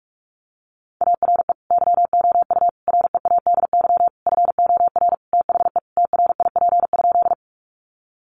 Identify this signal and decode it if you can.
Morse code, heard as a keyed tone, about 44 wpm, 700 Hz. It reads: ARE YOU READY FOR THE TRIP?
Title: Sentences from Top 1300 Words 45wpm